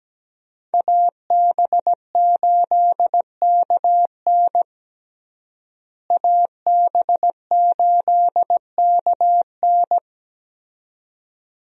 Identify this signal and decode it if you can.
Morse code, heard as a keyed tone, about 17 wpm, 700 Hz. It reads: AB8KN AB8KN
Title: USA Call Signs - 5 Characters - Repeated 2x - 17wpm